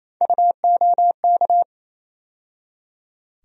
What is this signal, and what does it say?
Morse code, heard as a keyed tone, about 28 wpm, 700 Hz. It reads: UOK
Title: USA Call Sign Postfixes - 3 Letters 28wpm